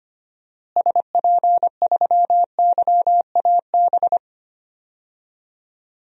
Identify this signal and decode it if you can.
Morse code, heard as a keyed tone, about 25 wpm, 700 Hz. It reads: SP3YAB
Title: DXCC-100 Call Signs - 6 Characters - Repeated 2x - 25wpm